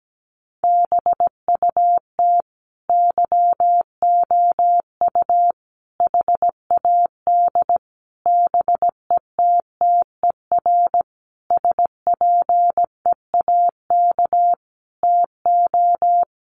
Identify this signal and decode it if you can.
Morse code, heard as a keyed tone, about 17 wpm, 700 Hz. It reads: BUT YOU HAD BETTER SPEAK TO
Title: Sentences from Top 1100 Words 17wpm